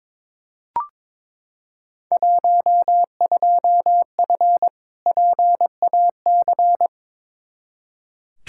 Morse code, heard as a keyed tone, about 22 wpm, 700 Hz. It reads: 12F PAC
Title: Field Day Exchanges Encore - 22wpm